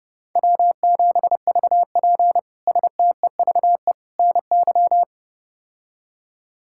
Morse code, heard as a keyed tone, about 30 wpm, 700 Hz. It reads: W7VP STEVE NY
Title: K1USN SST QSO Practice - 30wpm